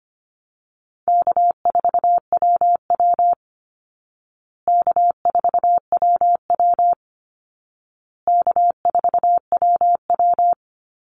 Morse code, heard as a keyed tone, about 25 wpm, 700 Hz. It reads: K4WW K4WW K4WW
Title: Easy Call Signs Repeated 3x - 25wpm